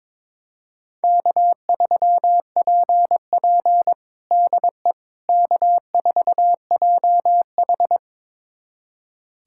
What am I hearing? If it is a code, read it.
Morse code, heard as a keyed tone, about 22 wpm, 700 Hz. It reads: K3PP DE K4JH
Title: USA Callsign DE Callsign - 4 Characters - 22wpm